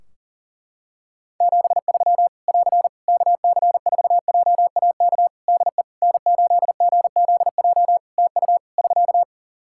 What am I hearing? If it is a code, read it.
Morse code, heard as a keyed tone, about 40 wpm, 700 Hz. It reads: 73 <AR> KC4JAK DE N8GZJ TU <SK>
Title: Practice QSO - Exchange 4 - Ending - 40wpm